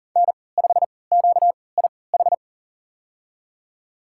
Morse code, heard as a keyed tone, about 40 wpm, spaced as for 23 wpm, 700 Hz. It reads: N5QIH
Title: USA Call Signs - 5 Characters - 3x Character Spacing - 40wpm